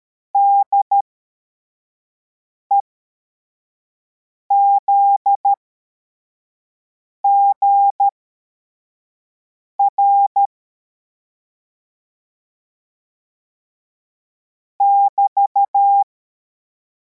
Morse code, heard as a keyed tone, about 13 wpm, spaced as for 4 wpm, 800 Hz. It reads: DEZGR <BT>
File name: formation / cw / Text44